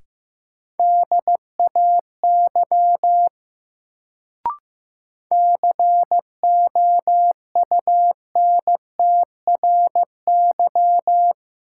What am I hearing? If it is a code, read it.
Morse code, heard as a keyed tone, about 15 wpm, 700 Hz. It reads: DAY COUNTRY
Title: Sets of 2 Words - New Word Drill - 15wpm